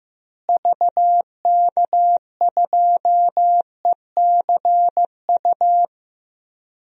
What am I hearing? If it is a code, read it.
Morse code, heard as a keyed tone, about 16 wpm, 700 Hz. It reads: VK2ECU
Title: DXCC-100 Call Signs - 6 Characters - Repeated 3x - 15wpm